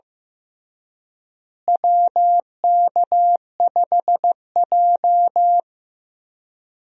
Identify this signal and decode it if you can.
Morse code, heard as a keyed tone, about 15 wpm, 700 Hz. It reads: WK5J
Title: USA Call Signs - 4 Characters - Repeated 3x - 15wpm